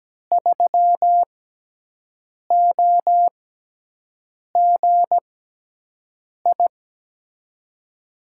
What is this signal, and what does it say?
Morse code, heard as a keyed tone, about 17 wpm, spaced as for 6 wpm, 700 Hz. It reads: 3OGI
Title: USA Call Signs - 6 Characters - 6x Character Spacing - 17wpm